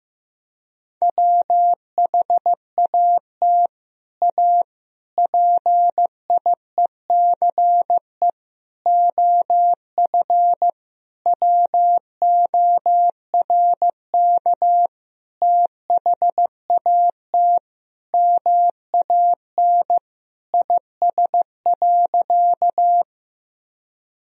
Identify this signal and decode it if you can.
Morse code, heard as a keyed tone, about 15 wpm, 700 Hz. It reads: WHAT A PIECE OF WORK THAT MAN IS.
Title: Sentences from Top 1000 Words - Review Part 2 - 15wpm